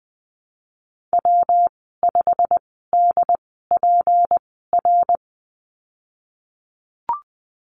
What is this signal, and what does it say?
Morse code, heard as a keyed tone, about 20 wpm, spaced as for 15 wpm, 700 Hz. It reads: W5DPR
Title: 213 Lesson 34g - Number 8 - Call Signs 20wpm